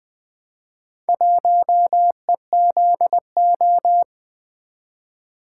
Morse code, heard as a keyed tone, about 20 wpm, 700 Hz. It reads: 1EZO
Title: Four Letter-Number Combinations - 20wpm